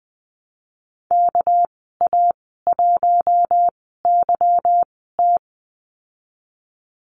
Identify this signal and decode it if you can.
Morse code, heard as a keyed tone, about 19 wpm, spaced as for 14 wpm, 700 Hz. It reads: KA1YT
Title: 213 Lesson 34g - Number 8 - Call Signs 20wpm